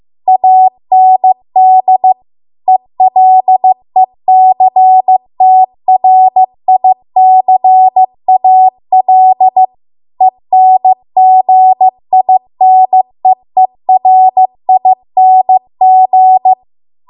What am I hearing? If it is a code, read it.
Morse code, heard as a keyed tone, about 15 wpm, 750 Hz. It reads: AND ELECTRICAL ENGINEERING
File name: Archive / 15 WPM / 15WPM